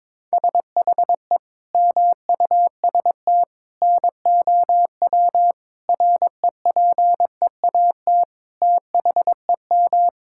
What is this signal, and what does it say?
Morse code, heard as a keyed tone, about 21 wpm, 700 Hz. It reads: SHE MUST NOW REPEAT THEM
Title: Sentences from Top 1600 Words 22wpm